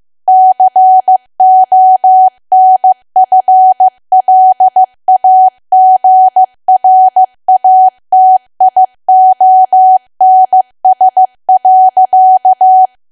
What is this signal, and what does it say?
Morse code, heard as a keyed tone, about 15 wpm, 750 Hz. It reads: CONFLAGRATIONS.
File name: cwa / W1AW-Level2 / 15WPM